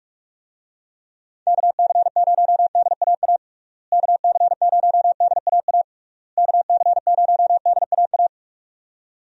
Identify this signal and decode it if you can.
Morse code, heard as a keyed tone, about 43 wpm, 700 Hz. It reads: KC0DAA KC0DAA KC0DAA
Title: USA Call Signs - 6 Characters - Repeated 3x - 45wpm